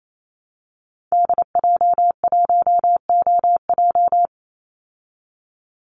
Morse code, heard as a keyed tone, about 27 wpm, 700 Hz. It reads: DJ1OJ
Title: Call Signs Repeated 2x - 28wpm